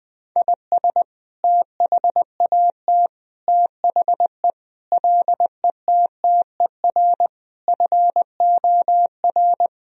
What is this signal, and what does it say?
Morse code, heard as a keyed tone, about 20 wpm, 700 Hz. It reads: IS THAT THE LETTER FOR
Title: Sentences from Top 1000 Words - Review Part 2 - 20wpm